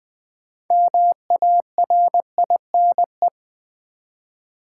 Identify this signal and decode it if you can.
Morse code, heard as a keyed tone, about 20 wpm, 700 Hz. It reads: MARINE
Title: Build A Word - 3 Letters At A Time - 6-Letter Words - 20wpm